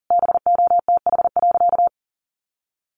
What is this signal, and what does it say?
Morse code, heard as a keyed tone, about 40 wpm, 700 Hz. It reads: BOTH.
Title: Sentences from Top 300 Words 40wpm